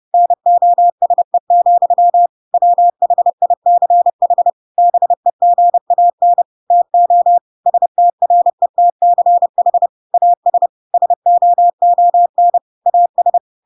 Morse code, heard as a keyed tone, about 30 wpm, 700 Hz. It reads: NOSE, WHICH BEGAN TO STRETCH AS SOON AS
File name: Pinocchio / Pinocchio ch3 30 30 ik1hge